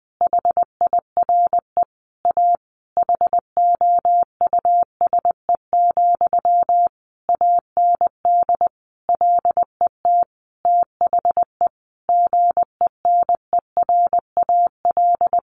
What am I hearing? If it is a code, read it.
Morse code, heard as a keyed tone, about 19 wpm, 700 Hz. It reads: HIRE A HOUSE, AND LET THE GENERAL
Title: Sentences from Top 1400 Words 20wpm